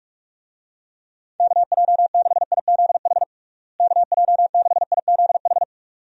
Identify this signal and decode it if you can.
Morse code, heard as a keyed tone, about 41 wpm, 700 Hz. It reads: KJ6IZH KJ6IZH
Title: USA Call Signs - 6 Characters - Repeated 2x - 45wpm